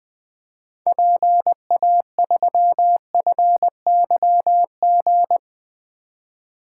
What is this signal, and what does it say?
Morse code, heard as a keyed tone, about 20 wpm, 700 Hz. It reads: PA3FYG
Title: 2023 SOTA Activator Call Signs - 20wpm